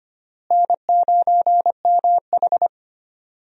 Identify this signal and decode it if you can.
Morse code, heard as a keyed tone, about 25 wpm, 700 Hz. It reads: N9MH
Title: USA Call Signs - 4 Characters - Repeated 3x - 25wpm